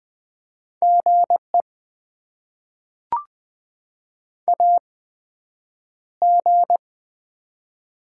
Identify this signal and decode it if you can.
Morse code, heard as a keyed tone, about 20 wpm, spaced as for 9 wpm, 700 Hz. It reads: GE AG
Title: Top 20 QSO Elements - 8x Character Spacing - 20wpm